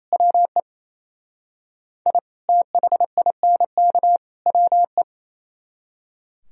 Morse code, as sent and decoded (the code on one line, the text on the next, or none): .-- . / .. / - .... .. -. -.- / .-- .
WE I THINK WE